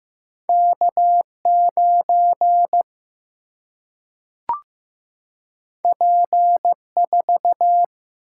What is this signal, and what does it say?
Morse code, heard as a keyed tone, about 15 wpm, 700 Hz. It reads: K 9 P 4
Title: Single-Letter Single-Number 15wpm